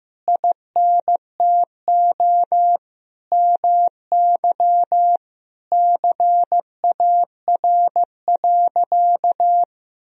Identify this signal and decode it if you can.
Morse code, heard as a keyed tone, about 15 wpm, 700 Hz. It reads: INTO MY CAR.
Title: Sentences from Top 1000 Words - Review Part 1 - 15wpm